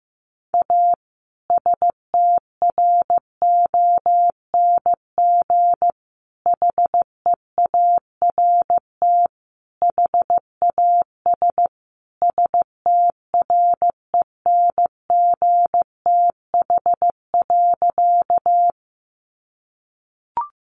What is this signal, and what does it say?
Morse code, heard as a keyed tone, about 15 wpm, 700 Hz. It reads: A STRONG HEART HAS STRENGTH.
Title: Sentences from Top 1300 Words 15wpm